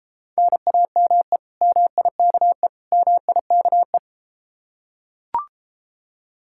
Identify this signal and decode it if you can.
Morse code, heard as a keyed tone, about 32 wpm, 700 Hz. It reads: NAME MIKE MIKE